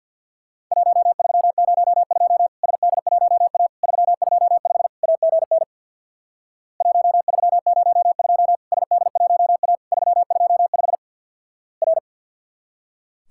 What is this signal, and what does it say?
Morse code, heard as a keyed tone, about 48 wpm, 700 Hz. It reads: 1302 SD1A 325 AGN 1302 SD1A 325 R
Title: Worked All Europe DX Contest - QTC Exchange with Repeats - 50wpm